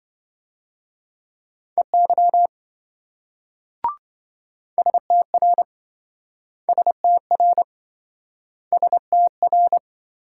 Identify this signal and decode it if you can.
Morse code, heard as a keyed tone, about 26 wpm, 700 Hz. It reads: EY STR STR STR